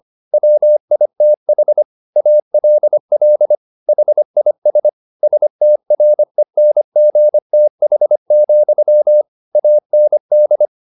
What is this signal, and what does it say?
Morse code, heard as a keyed tone, about 25 wpm, 600 Hz. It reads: WITH ALL HIS STRENGTH, AND